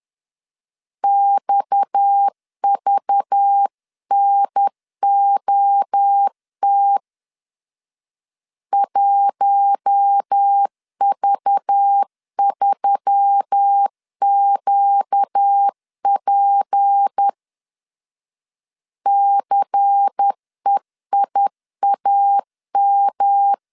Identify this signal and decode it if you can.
Morse code, heard as a keyed tone, about 10 wpm, 800 Hz. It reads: XVNOT 1V3QP CEIAM
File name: CW / DET / 1 6 1500C S B 10wpm 4137